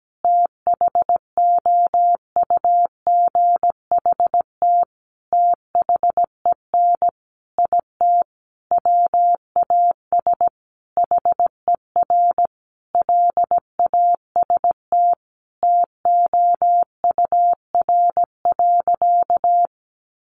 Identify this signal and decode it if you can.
Morse code, heard as a keyed tone, about 17 wpm, 700 Hz. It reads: THOUGHT THEN IT WAS HER LAST TOUR.
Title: Sentences from Top 1400 Words 17wpm